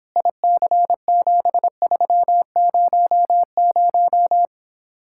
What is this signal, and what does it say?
Morse code, heard as a keyed tone, about 26 wpm, 700 Hz. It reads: IC7300